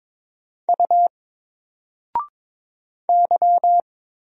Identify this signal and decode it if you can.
Morse code, heard as a keyed tone, about 22 wpm, 700 Hz. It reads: U Y
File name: Single-Letter-Number-22wpm